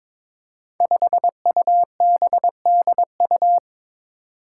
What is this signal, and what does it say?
Morse code, heard as a keyed tone, about 22 wpm, 700 Hz. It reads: 5UBDU
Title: Five Challenging Letter-Number Combinations 22wpm